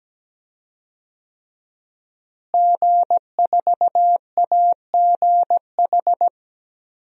Morse code, heard as a keyed tone, about 17 wpm, 700 Hz. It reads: G4AGH
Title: DXCC-100 Call Signs - 5 Characters - Repeated 3x - 17wpm